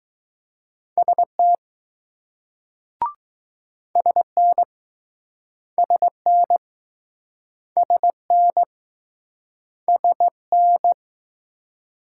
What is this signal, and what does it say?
Morse code, heard as a keyed tone, about 20 wpm, 700 Hz. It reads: ST SN SN SN SN